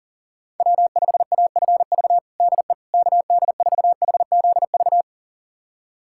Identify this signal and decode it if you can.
Morse code, heard as a keyed tone, about 37 wpm, 700 Hz. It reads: W5AFV DE KD4HZV
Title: USA Callsign DE Callsign 40wpm